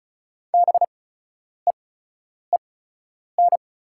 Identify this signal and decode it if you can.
Morse code, heard as a keyed tone, about 36 wpm, spaced as for 10 wpm, 690 Hz. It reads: BEEN
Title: 8x Character Spacing - Top 300 Words - 35wpm